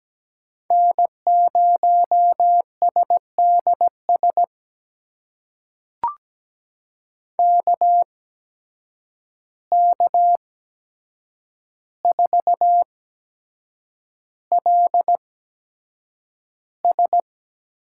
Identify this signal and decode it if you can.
Morse code, heard as a keyed tone, about 17 wpm, spaced as for 7 wpm, 700 Hz. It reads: N0SDS KK4LS
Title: USA Call Signs - 5 Characters - 8x Character Spacing - 17wpm